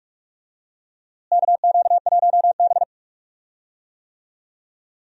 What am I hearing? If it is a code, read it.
Morse code, heard as a keyed tone, about 45 wpm, 700 Hz. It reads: KQ1B
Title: USA Call Signs - 4 Characters - Repeated 2x - 45wpm